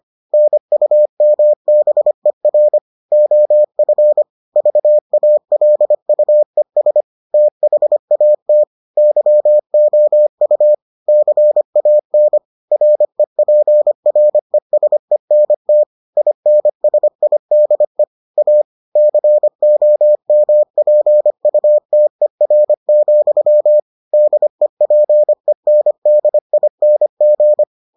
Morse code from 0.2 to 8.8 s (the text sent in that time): NUMBER OF VALUES THAT